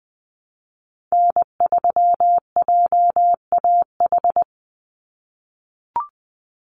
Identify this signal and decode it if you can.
Morse code, heard as a keyed tone, about 20 wpm, 700 Hz. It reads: N3JAH
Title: USA Call Signs 20wpm